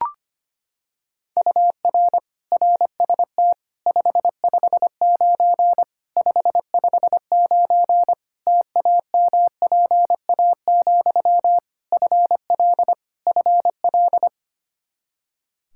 Morse code, heard as a keyed tone, about 25 wpm, 700 Hz. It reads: UR RST 559 559 TAMPA, FL FL
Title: Practice QSO - Exchange 1 - Beginning - 25wpm